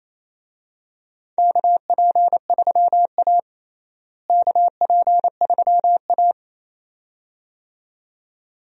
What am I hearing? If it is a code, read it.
Morse code, heard as a keyed tone, about 28 wpm, 700 Hz. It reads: KP3A KP3A
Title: USA Call Signs - 4 Characters - Repeated 2x - 28wpm